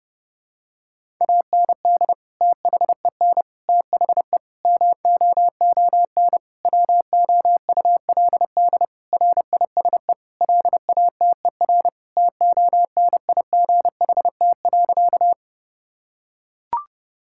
Morse code, heard as a keyed tone, about 30 wpm, 700 Hz. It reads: AND THEN THE MOON WOULD RISE LATER TONIGHT.
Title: Sentences from Top 1000 Words - Review Part 6 - 30wpm